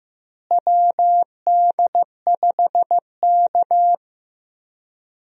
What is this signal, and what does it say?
Morse code, heard as a keyed tone, about 15 wpm, 700 Hz. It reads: WD5K
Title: USA Call Signs - 4 Characters - Repeated 3x - 15wpm